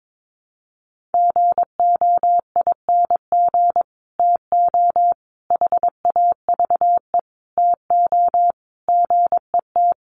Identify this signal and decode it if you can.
Morse code, heard as a keyed tone, about 22 wpm, 700 Hz. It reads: GOING TO HAVE TO GET
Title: Sets of 5 Words Encore2 22wpm